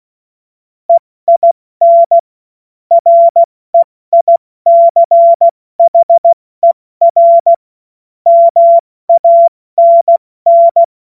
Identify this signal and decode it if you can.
Morse code, heard as a keyed tone, about 16 wpm, 680 Hz. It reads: EIN REICHER MANN
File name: Morsekurs / Witze / Witz 06. September 2022 70 F80